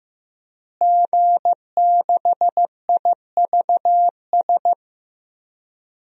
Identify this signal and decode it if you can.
Morse code, heard as a keyed tone, about 15 wpm, 700 Hz. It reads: G6IVS
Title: DXCC-100 Call Signs - 5 Characters - Repeated 3x - 15wpm